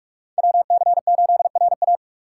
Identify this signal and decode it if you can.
Morse code, heard as a keyed tone, about 46 wpm, 700 Hz. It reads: WC8RA